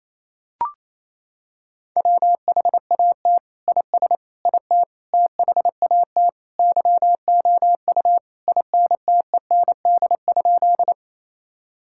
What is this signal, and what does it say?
Morse code, heard as a keyed tone, about 28 wpm, 700 Hz. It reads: WHAT IS IT THAT YOU INTEND?
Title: Sentences from Top 1600 Words 28wpm